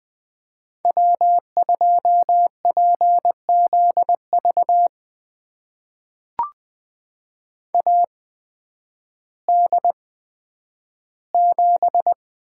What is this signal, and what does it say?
Morse code, heard as a keyed tone, about 20 wpm, spaced as for 11 wpm, 700 Hz. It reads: W2PZV AD7
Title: USA Call Signs - 5 Characters - 8x Character Spacing - 20wpm